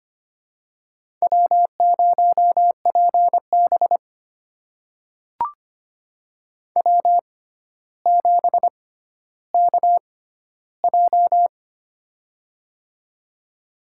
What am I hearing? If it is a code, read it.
Morse code, heard as a keyed tone, about 25 wpm, spaced as for 13 wpm, 700 Hz. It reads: W0PB W7KJ
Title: USA Call Signs - 6x Character Spacing - 25wpm